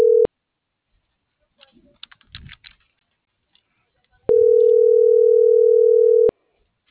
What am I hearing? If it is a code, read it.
Ringback tone (North America), 440 + 480 Hz.